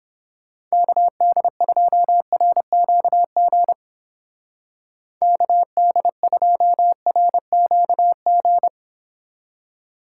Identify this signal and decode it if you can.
Morse code, heard as a keyed tone, about 31 wpm, 700 Hz. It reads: KD2RQG KD2RQG